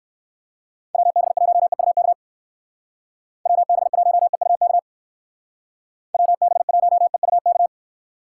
Morse code, heard as a keyed tone, about 57 wpm, 700 Hz. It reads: WB1EUK WB1EUK WB1EUK